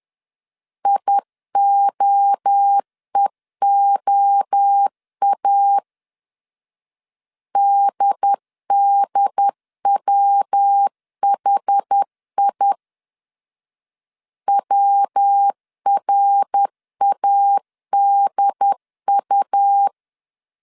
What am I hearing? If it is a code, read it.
Morse code, heard as a keyed tone, about 10 wpm, 800 Hz. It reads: IOEOA DDWHI WRADU
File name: CW / DET / 1 3 1500C S B 10wpm 3335